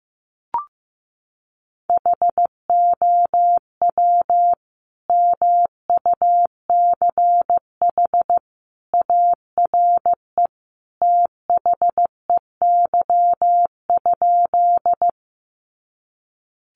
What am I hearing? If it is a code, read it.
Morse code, heard as a keyed tone, about 15 wpm, 700 Hz. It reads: HOW MUCH ARE THEY?
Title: Sentences from Top 200 Words 15wpm